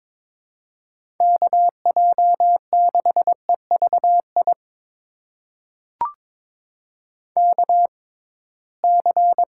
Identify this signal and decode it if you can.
Morse code, heard as a keyed tone, about 21 wpm, spaced as for 16 wpm, 700 Hz. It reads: KJ6EVI KC
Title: USA Call Signs - 6 Characters - 6x Character Spacing - 22wpm